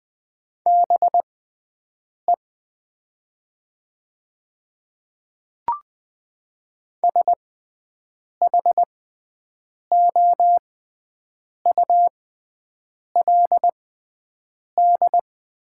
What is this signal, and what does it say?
Morse code, heard as a keyed tone, about 20 wpm, spaced as for 7 wpm, 700 Hz. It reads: BE SHOULD
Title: 6x Character Spacing - Top 300 Words - 20wpm